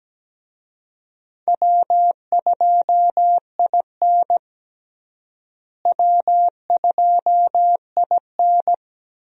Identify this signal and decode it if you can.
Morse code, heard as a keyed tone, about 17 wpm, 700 Hz. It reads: W2IN W2IN
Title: USA Call Signs - 4 Characters - Repeated 2x - 17wpm